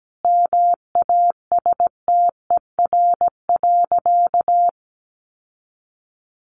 Morse code, heard as a keyed tone, about 17 wpm, 700 Hz. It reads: MASTER.
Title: Sentences from Top 1300 Words 17wpm